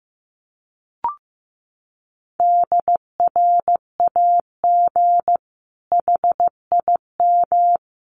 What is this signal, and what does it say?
Morse code, heard as a keyed tone, about 15 wpm, 700 Hz. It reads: DRAG HIM
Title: Sentences from Top 1500 Words 15wpm